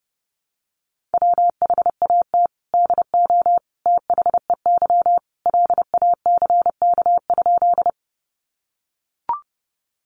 Morse code, heard as a keyed tone, about 30 wpm, 700 Hz. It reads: WHAT DO THEY LACK?
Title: Sentences from Top 1100 Words 30wpm